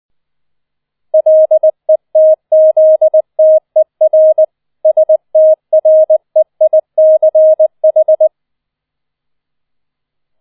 Morse code, heard as a keyed tone, about 20 wpm, 630 Hz. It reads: LETZTER STREICH